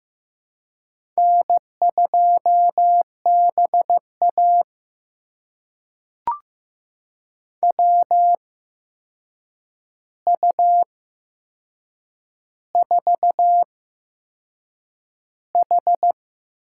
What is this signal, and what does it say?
Morse code, heard as a keyed tone, about 15 wpm, spaced as for 6 wpm, 700 Hz. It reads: N2BA WU4H
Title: USA Call Signs - 4 Characters - 8x Character Spacing - 15wpm